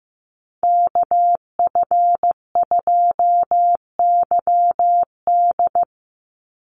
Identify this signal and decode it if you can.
Morse code, heard as a keyed tone, about 15 wpm, 700 Hz. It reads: KF2YD
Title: USA Call Signs 15wpm